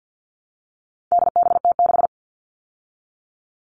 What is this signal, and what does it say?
Morse code, heard as a keyed tone, about 50 wpm, 700 Hz. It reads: DBT6
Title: All Cut Numbers - 4-Digit 50wpm